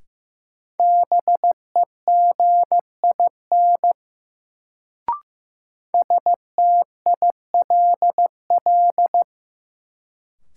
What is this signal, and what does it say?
Morse code, heard as a keyed tone, about 15 wpm, 700 Hz. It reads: BEGIN STILL
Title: Sentences from Top 300 Words - New Word Drill - 15wpm